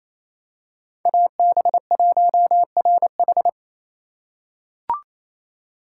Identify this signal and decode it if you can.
Morse code, heard as a keyed tone, about 28 wpm, 700 Hz. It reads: AB1RH
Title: USA Call Signs - 5 Characters - 2x Character Spacing - 28wpm